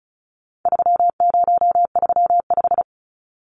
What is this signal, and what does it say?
Morse code, heard as a keyed tone, about 34 wpm, 700 Hz. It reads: 3035
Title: Numbers - 4-Digit 35wpm